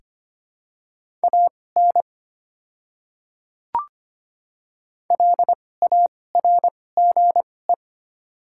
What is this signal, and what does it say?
Morse code, heard as a keyed tone, about 25 wpm, spaced as for 18 wpm, 700 Hz. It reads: AN LARGE
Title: 2x Character Spacing - Top 200 Words - 25wpm